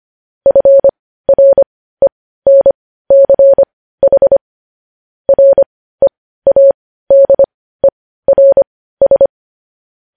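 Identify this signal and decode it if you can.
Morse code, heard as a keyed tone, about 25 wpm, spaced as for 15 wpm, 550 Hz. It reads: FRENCH READERS
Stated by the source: Morse code news headlines at 15.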